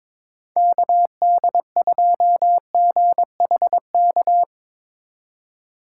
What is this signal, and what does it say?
Morse code, heard as a keyed tone, about 22 wpm, 700 Hz. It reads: KD2GHK
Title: USA Call Signs - 4x Character Spacing - 22wpm